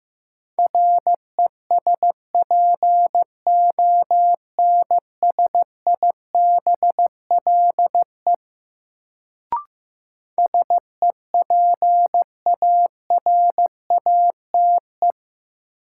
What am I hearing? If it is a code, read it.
Morse code, heard as a keyed tone, about 15 wpm, 700 Hz. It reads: RESPONSIBLE SEPARATE
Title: Sentences from Top 1000 Words - Review Part 7 - New Word Drill - 15wpm